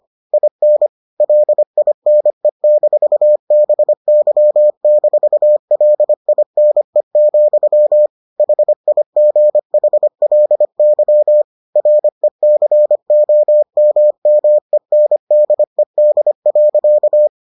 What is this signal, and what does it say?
Morse code, heard as a keyed tone, about 25 wpm, 600 Hz. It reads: IN LINE-BY-LINE, HIGHLY RECOMMENDED.